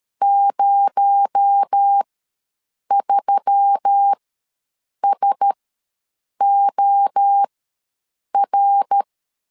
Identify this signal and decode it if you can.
Morse code, heard as a keyed tone, about 13 wpm, spaced as for 7 wpm, 800 Hz. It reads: 03SOR
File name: LEZIONI / 6 PKX0 / 6 PKX0 ws 12wpm 3448